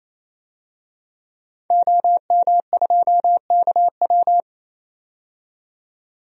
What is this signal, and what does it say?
Morse code, heard as a keyed tone, about 28 wpm, 700 Hz. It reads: OM2KW
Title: DXCC-100 Call Signs - 5 Characters - 28wpm